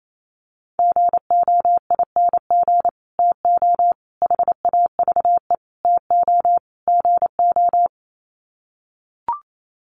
Morse code, heard as a keyed tone, about 28 wpm, 700 Hz. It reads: GOING TO HAVE TO GO
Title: Sets of 5 Words Encore 28wpm